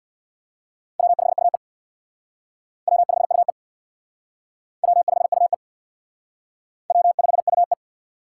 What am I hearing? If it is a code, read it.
Morse code, heard as a keyed tone, about 59 wpm, 700 Hz. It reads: W5UE W5UE W5UE W5UE
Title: Speed Racing - USA 1x2 Call Signs 50wpm